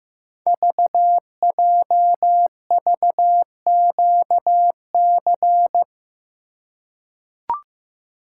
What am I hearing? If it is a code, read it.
Morse code, heard as a keyed tone, about 15 wpm, 700 Hz. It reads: VJVQC
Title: Code Groups - 5 Letters - 15wpm